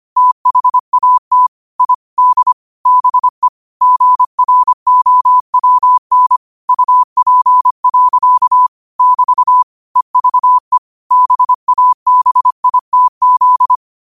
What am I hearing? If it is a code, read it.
Morse code, heard as a keyed tone, about 25 wpm, 1000 Hz. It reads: THAT I D BE GROWN UP. <BT> EVE BABITZ
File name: 25wpm